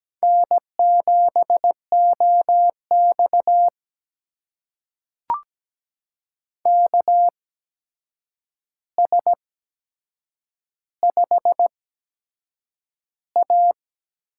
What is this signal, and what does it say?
Morse code, heard as a keyed tone, about 17 wpm, spaced as for 7 wpm, 700 Hz. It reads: N7OX KS5A
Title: USA Call Signs - 4 Characters - 8x Character Spacing - 17wpm